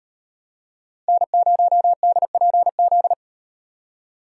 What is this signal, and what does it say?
Morse code, heard as a keyed tone, about 38 wpm, 700 Hz. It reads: N0DPZ